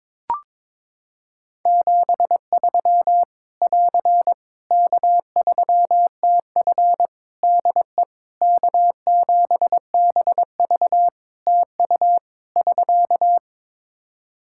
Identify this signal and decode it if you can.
Morse code, heard as a keyed tone, about 22 wpm, 700 Hz. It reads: 73 <AR> K3TF DE K7BV TU <SK>
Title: Practice QSO - Exchange 4 - Ending - 22wpm